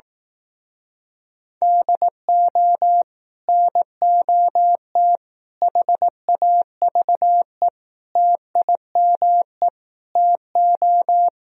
Morse code, heard as a keyed tone, about 18 wpm, 700 Hz. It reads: DO NOT HAVE TIME TO
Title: Speed-Racing - Sets of 5 Words 15wpm